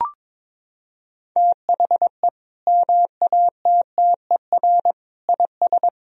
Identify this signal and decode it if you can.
Morse code, heard as a keyed tone, about 22 wpm, 700 Hz. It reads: THE MATTER IS
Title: Sentences from Top 1000 Words - Review Part 4 - 22wpm